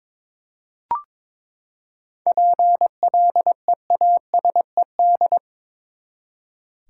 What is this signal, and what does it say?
Morse code, heard as a keyed tone, about 22 wpm, 700 Hz. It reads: PLEASED
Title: Seven Letter Words - 22wpm